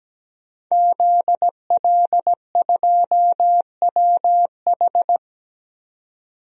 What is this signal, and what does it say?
Morse code, heard as a keyed tone, about 17 wpm, 700 Hz. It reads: ZL2WH
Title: DXCC-100 Call Signs - 5 Characters - 17wpm